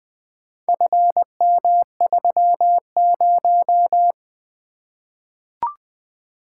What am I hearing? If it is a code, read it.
Morse code, heard as a keyed tone, about 20 wpm, 700 Hz. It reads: FM30
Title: Four Letter-Number Combinations - 20wpm